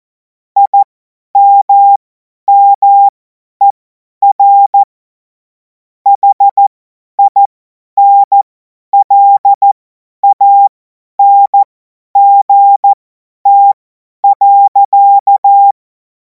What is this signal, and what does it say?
Morse code, heard as a keyed tone, about 14 wpm, spaced as for 10 wpm, 800 Hz. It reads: IMMER HINLANGT.
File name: Morsekurs / Witze / Witz 24. Februar 2026 50 F70